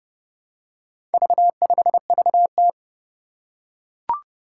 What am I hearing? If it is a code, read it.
Morse code, heard as a keyed tone, about 30 wpm, 700 Hz. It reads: V5VT
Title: Four Challenging Letter-Number Combinations 30wpm